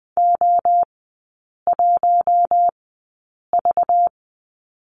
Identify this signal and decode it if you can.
Morse code, heard as a keyed tone, about 20 wpm, spaced as for 8 wpm, 700 Hz. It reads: O1V
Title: 191 Lesson 31e - Letter v - Three Characters 20wpm